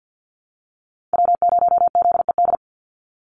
Morse code, heard as a keyed tone, about 50 wpm, 700 Hz. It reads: W0ZED
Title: USA Call Signs 50wpm